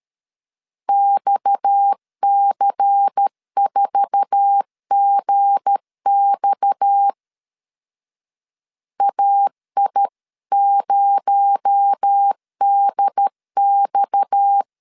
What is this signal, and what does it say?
Morse code, heard as a keyed tone, about 13 wpm, 800 Hz. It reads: XC4GX AI0DX
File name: CW / DET / 1 9 1500C S A 12wpm 3800